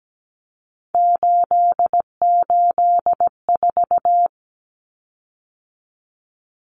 Morse code, heard as a keyed tone, about 17 wpm, 700 Hz. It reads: 884
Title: Common Cut Numbers - 3-Digit 17wpm